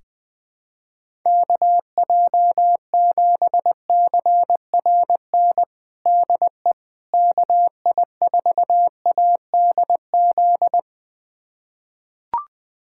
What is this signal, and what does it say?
Morse code, heard as a keyed tone, about 20 wpm, 700 Hz. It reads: KJ7CRN DE KI4ADZ
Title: USA Callsign DE Callsign - 6 Characters - 20wpm